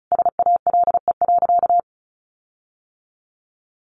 Morse code, heard as a keyed tone, about 34 wpm, 700 Hz. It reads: SALE.
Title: Sentences from Top 1400 Words 35wpm